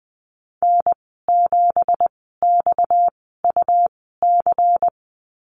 Morse code, heard as a keyed tone, about 20 wpm, spaced as for 15 wpm, 700 Hz. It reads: N7XUC
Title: 238 Lesson 38g - Letter q - Call Signs 20wpm